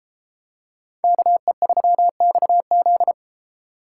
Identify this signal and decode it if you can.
Morse code, heard as a keyed tone, about 33 wpm, 700 Hz. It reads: KE3XZ